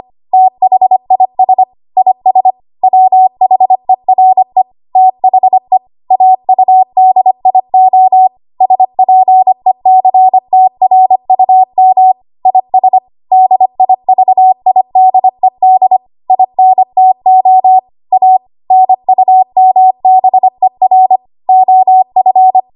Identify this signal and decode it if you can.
Morse code, heard as a keyed tone, about 25 wpm, 750 Hz. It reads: THIS IS WHERE THE AUDIO SPECTRUM IS DIVIDED INTO A NUMBER OF
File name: Morse / 25WPM